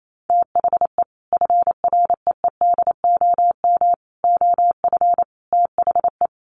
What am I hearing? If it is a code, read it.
Morse code, heard as a keyed tone, about 28 wpm, 700 Hz. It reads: THE FREEDOM OF THE
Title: Sentences from Top 800 Words 28wpm